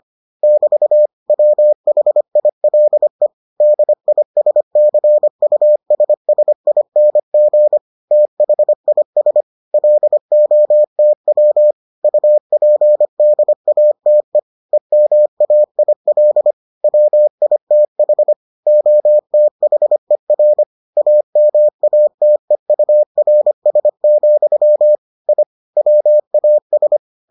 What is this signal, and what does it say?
Morse code, heard as a keyed tone, about 25 wpm, 600 Hz. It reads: <BT> WHILE DISCUSSING THIS LOTW UPDATE EMAIL WITH OTHER AMATEURS, I WAS